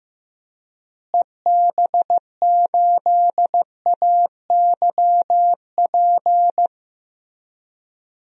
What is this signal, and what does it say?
Morse code, heard as a keyed tone, about 15 wpm, 700 Hz. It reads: EB8AYP
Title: DXCC-100 Call Signs - 6 Characters - Repeated 3x - 15wpm